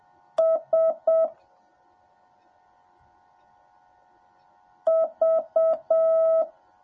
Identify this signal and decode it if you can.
Morse code, heard as a keyed tone, about 7 wpm, 650 Hz. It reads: S V
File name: jh8cbh / 2021cw / cw10a